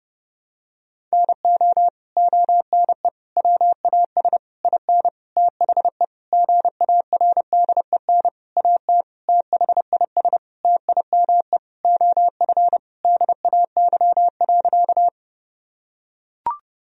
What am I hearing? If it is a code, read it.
Morse code, heard as a keyed tone, about 28 wpm, 700 Hz. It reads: NO ONE WAS IN THE GARDEN AT THIS TIME OF DAY.
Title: Sentences from Top 1000 Words - Review Part 4 - 30wpm